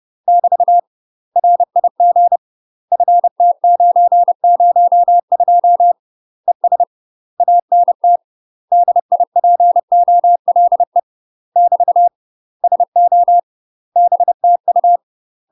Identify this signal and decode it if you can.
Morse code, heard as a keyed tone, about 30 wpm, 700 Hz. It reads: <BT> RIG FT902 ES ANT DIPOLE <BT> SO BTU
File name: UK MASTER QSOs TRAINING / 30WPM / UK MASTER QSOs 0001